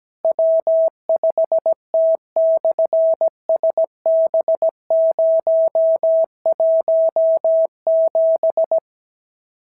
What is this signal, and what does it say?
Morse code, heard as a keyed tone, about 17 wpm, 650 Hz. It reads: W5T/SB017
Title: SOTA QSO Practice 17wpm